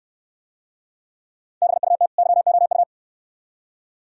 Morse code, heard as a keyed tone, about 62 wpm, 700 Hz. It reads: BUT YOU